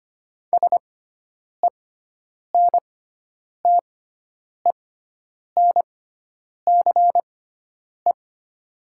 Morse code, heard as a keyed tone, about 25 wpm, spaced as for 9 wpm, 700 Hz. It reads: SENTENCE
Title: 6x Character Spacing - Top 200 Words - 25wpm